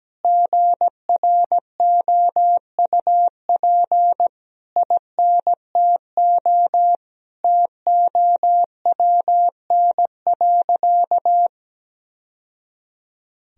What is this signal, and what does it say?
Morse code, heard as a keyed tone, about 17 wpm, 700 Hz. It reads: GROUP INTO TOWN.
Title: Sentences from Top 1000 Words - Review Part 1 - 17wpm